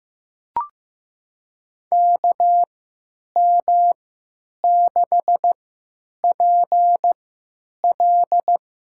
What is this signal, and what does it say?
Morse code, heard as a keyed tone, about 16 wpm, spaced as for 9 wpm, 700 Hz. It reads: KM6PL
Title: USA Call Signs - 3x Character Spacing - 15wpm